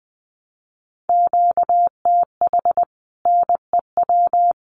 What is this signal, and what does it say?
Morse code, heard as a keyed tone, about 20 wpm, 700 Hz. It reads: QTH NEW
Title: Common QSO Phrases Encore 20wpm